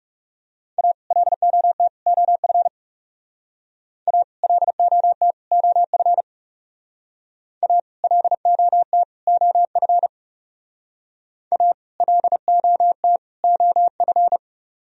Morse code, heard as a keyed tone, about 38 wpm, 700 Hz. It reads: A LOT OF A LOT OF A LOT OF A LOT OF